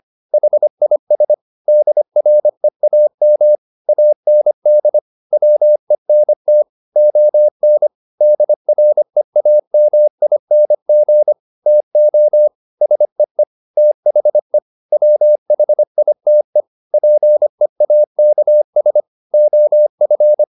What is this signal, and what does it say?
Morse code, heard as a keyed tone, about 25 wpm, 600 Hz. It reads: HIS DREAM AND WENT ON DREAMING TO SEE THE WHITE PEAKS OF
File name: the old man and the sea part 1 25wpm